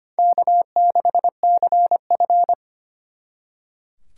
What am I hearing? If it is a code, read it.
Morse code, heard as a keyed tone, about 33 wpm, 700 Hz. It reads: K6CF